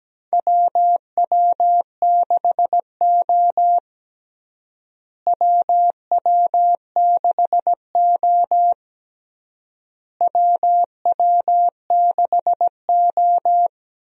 Morse code, heard as a keyed tone, about 17 wpm, 700 Hz. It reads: WW6O WW6O WW6O
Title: USA Call Signs - 4 Characters - Repeated 3x - 17wpm